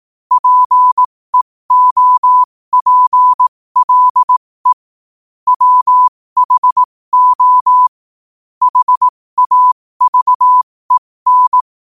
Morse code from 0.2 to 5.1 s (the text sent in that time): PEOPLE